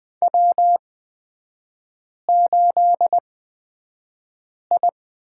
Morse code, heard as a keyed tone, about 20 wpm, spaced as for 5 wpm, 690 Hz. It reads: W8I
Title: 236 Lesson 38e - Letter q - Three Characters 20wpm